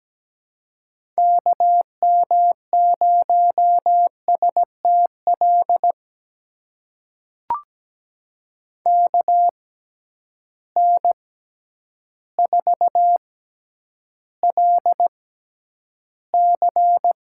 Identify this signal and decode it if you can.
Morse code, heard as a keyed tone, about 17 wpm, spaced as for 9 wpm, 700 Hz. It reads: KM0STL KN4LC
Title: USA Call Signs - 6x Character Spacing - 17wpm